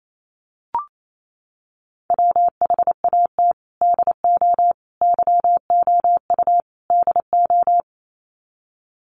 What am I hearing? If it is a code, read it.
Morse code, heard as a keyed tone, about 28 wpm, 700 Hz. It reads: WHAT DO YOU DO
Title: Sets of 4 Words 28wpm